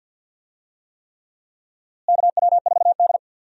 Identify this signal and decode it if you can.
Morse code, heard as a keyed tone, about 47 wpm, 700 Hz. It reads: KW4D
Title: USA Call Signs - 4 Characters - Repeated 3x - 50wpm